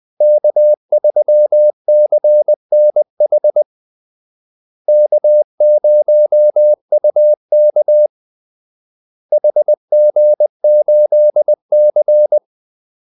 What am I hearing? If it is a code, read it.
Morse code, heard as a keyed tone, about 20 wpm, 600 Hz. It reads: K3CNH K0UK HG8C
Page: Kaldesignaler 20 wpm